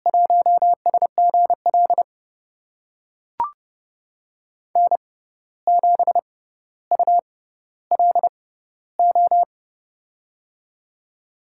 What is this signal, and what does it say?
Morse code, heard as a keyed tone, about 32 wpm, spaced as for 15 wpm, 700 Hz. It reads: 1SGL N7ULO
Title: USA Call Signs - 5 Characters - 6x Character Spacing - 30wpm